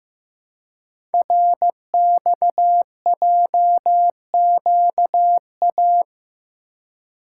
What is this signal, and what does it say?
Morse code, heard as a keyed tone, about 15 wpm, 700 Hz. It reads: RXJQA
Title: Code Groups - 5 Letters - 15wpm